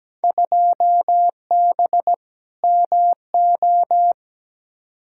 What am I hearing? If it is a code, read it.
Morse code, heard as a keyed tone, about 17 wpm, 700 Hz. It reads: 2B MO
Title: Field Day Exchanges Encore - 17wpm